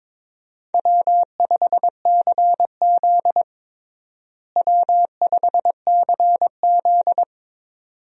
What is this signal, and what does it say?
Morse code, heard as a keyed tone, about 22 wpm, 700 Hz. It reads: W5CZ W5CZ
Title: USA Call Signs - 4 Characters - Repeated 2x - 22wpm